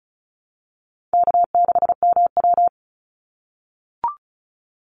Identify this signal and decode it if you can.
Morse code, heard as a keyed tone, about 34 wpm, 700 Hz. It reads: K6MW
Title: NA SOTA Chaser Callsigns 35wpm